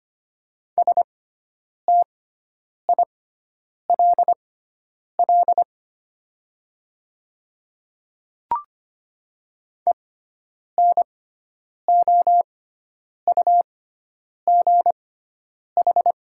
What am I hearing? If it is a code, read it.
Morse code, heard as a keyed tone, about 25 wpm, spaced as for 9 wpm, 690 Hz. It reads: STILL ENOUGH
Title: 6x Character Spacing - Top 500 Words - 25wpm